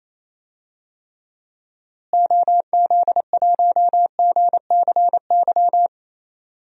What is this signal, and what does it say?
Morse code, heard as a keyed tone, about 28 wpm, 700 Hz. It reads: OZ1GCY
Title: DXCC-100 Call Signs - 6 Characters - Repeated 3x - 28wpm